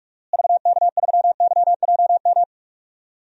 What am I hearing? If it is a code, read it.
Morse code, heard as a keyed tone, about 48 wpm, 700 Hz. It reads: VK3YJM
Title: DXCC-100 Call Signs - 6 Characters - Repeated 2x - 45wpm